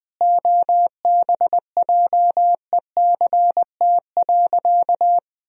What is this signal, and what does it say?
Morse code, heard as a keyed tone, about 20 wpm, 700 Hz. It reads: OBJECT.
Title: Sentences from Top 1000 Words - Review Part 3 - 20wpm